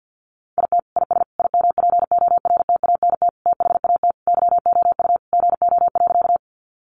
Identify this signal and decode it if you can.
Morse code, heard as a keyed tone, about 50 wpm, 700 Hz. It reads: IT IS IMPORTANT THAT YOU GO.
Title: Sentences from Top 600 Words 50wpm